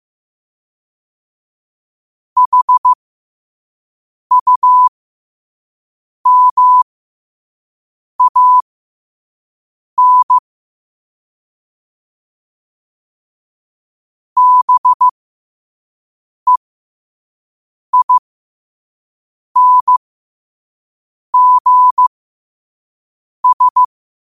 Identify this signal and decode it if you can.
Morse code, heard as a keyed tone, about 15 wpm, spaced as for 5 wpm, 1000 Hz. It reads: HUMAN BEINGS
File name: qotdfiles / 5wpm